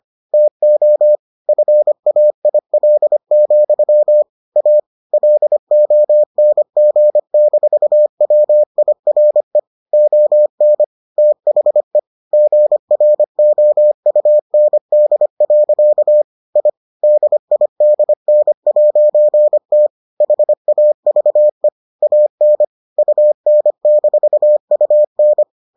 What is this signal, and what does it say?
Morse code, heard as a keyed tone, about 25 wpm, 600 Hz. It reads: TO FAIL, A LONG-WIRE ON THE GROUND. I DIDN'T HAVE AN UN-UN